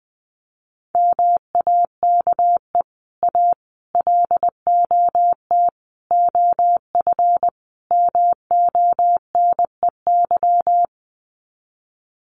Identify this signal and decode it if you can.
Morse code, heard as a keyed tone, about 20 wpm, 700 Hz. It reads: MAKE A LOT OF MONEY
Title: Sets of 5 Words Encore2 20wpm